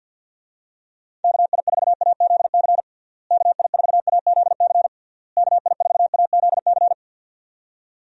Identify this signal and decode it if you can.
Morse code, heard as a keyed tone, about 47 wpm, 700 Hz. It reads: KI4AZC KI4AZC KI4AZC
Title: USA Call Signs - 6 Characters - Repeated 3x - 50wpm